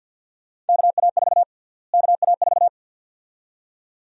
Morse code, heard as a keyed tone, about 52 wpm, 700 Hz. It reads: KA4 KA4
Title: USA Call Sign Prefixes - Repeated 2x - 50wpm